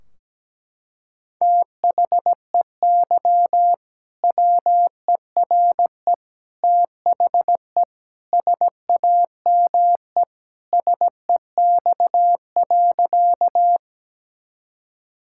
Morse code, heard as a keyed tone, about 17 wpm, 700 Hz. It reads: THEY WERE THE SAME SEX.
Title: Sentences from Top 1600 Words 17wpm